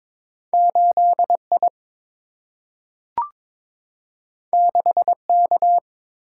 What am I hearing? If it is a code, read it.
Morse code, heard as a keyed tone, about 22 wpm, 700 Hz. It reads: 8I 6K
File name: practice / 22wpm / Two Letter-Number Combinations - 22wpm